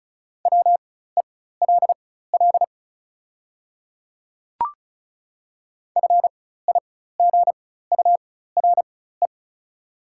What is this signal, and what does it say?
Morse code, heard as a keyed tone, about 34 wpm, spaced as for 16 wpm, 690 Hz. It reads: WELL FIGURE
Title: 4x Character Spacing - Top 500 Words - 35wpm